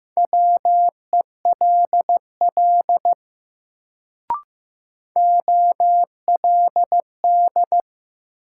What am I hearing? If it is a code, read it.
Morse code, heard as a keyed tone, about 15 wpm, 700 Hz. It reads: WELL OLD
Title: Sentences from Top 200 Words - New Word Drill - 15wpm